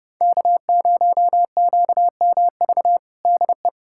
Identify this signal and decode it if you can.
Morse code, heard as a keyed tone, about 31 wpm, 700 Hz. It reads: K0QMV DE
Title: USA Callsign DE Callsign - 5 Characters - 30wpm